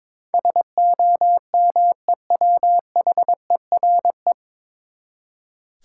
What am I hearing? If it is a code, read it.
Morse code, heard as a keyed tone, about 22 wpm, 700 Hz. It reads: SOMEWHERE
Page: Morse Code Ninja practice 22wpm